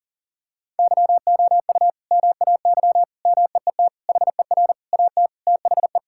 Morse code, heard as a keyed tone, about 37 wpm, 700 Hz. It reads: YOU MAY MEET HER AT THE
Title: Sentences from Top 1600 Words 40wpm